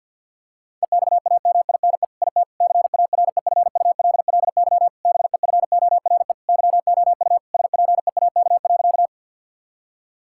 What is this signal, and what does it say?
Morse code, heard as a keyed tone, about 50 wpm, 700 Hz. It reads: EXAMINE IT CAREFULLY BEFORE YOU SPEAK.
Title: Sentences from Top 1600 Words 50wpm